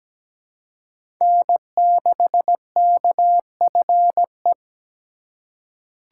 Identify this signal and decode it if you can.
Morse code, heard as a keyed tone, about 17 wpm, 700 Hz. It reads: N6KFE
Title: USA Call Signs - 5 Characters - 4x Character Spacing - 17wpm